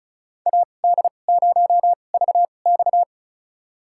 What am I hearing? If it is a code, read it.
Morse code, heard as a keyed tone, about 34 wpm, spaced as for 25 wpm, 700 Hz. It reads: AD0VX
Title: USA Call Signs - 5 Characters - 2x Character Spacing - 35wpm